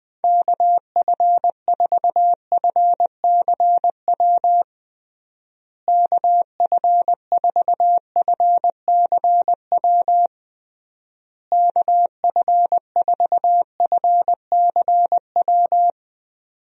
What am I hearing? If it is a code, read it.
Morse code, heard as a keyed tone, about 20 wpm, 700 Hz. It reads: KF4FCW KF4FCW KF4FCW
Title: USA Call Signs - 6 Characters - Repeated 3x - 20wpm